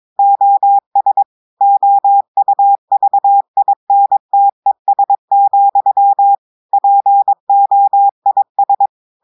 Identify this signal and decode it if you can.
Morse code, heard as a keyed tone, about 22 wpm, 800 Hz. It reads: OS OUVINTES, POIS
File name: audios / portugais / portugais texte en clair 2 vitesse 22 mots minutes